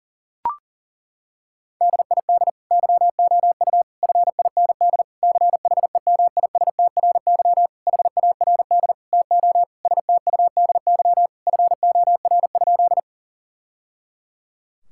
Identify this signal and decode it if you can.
Morse code, heard as a keyed tone, about 40 wpm, 700 Hz. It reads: DID YOU FIND CHEMISTRY HARD TO STUDY FOR?
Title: Sentences from Top 1000 Words - Review Part 6 - 40wpm